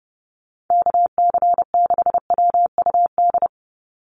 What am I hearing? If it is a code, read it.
Morse code, heard as a keyed tone, about 30 wpm, 700 Hz. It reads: KC6WUD
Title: USA Call Signs 30wpm